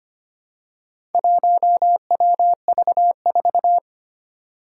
Morse code, heard as a keyed tone, about 25 wpm, 700 Hz. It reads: 1WV4
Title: Four Challenging Letter-Number Combinations 25wpm